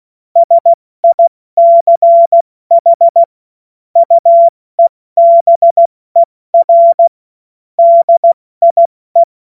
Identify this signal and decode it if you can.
Morse code, heard as a keyed tone, about 16 wpm, 680 Hz. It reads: SICH UEBER DIE
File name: Witz-14-Oktober-2025-70-F80